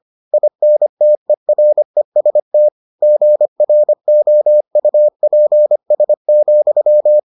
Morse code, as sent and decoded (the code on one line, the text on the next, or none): .. -. - . .-. ... - / --. .-. --- ..- .--. ... --..--
INTEREST GROUPS,